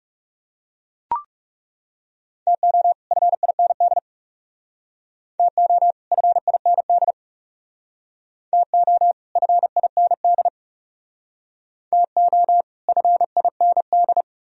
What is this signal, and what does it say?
Morse code, heard as a keyed tone, about 39 wpm, 700 Hz. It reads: TO FIND TO FIND TO FIND TO FIND